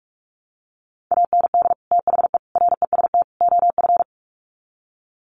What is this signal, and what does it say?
Morse code, heard as a keyed tone, about 44 wpm, 700 Hz. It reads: AND THE REST OF
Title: Sets of 4 Words 45wpm